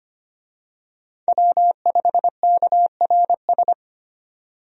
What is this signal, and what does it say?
Morse code, heard as a keyed tone, about 25 wpm, 700 Hz. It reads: W5KRS
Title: USA Call Signs - 5 Characters - 25wpm